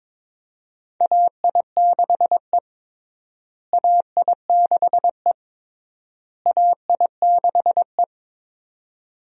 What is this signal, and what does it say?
Morse code, heard as a keyed tone, about 22 wpm, 700 Hz. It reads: AI6E AI6E AI6E
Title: USA Call Signs - 4 Characters - Repeated 3x - 22wpm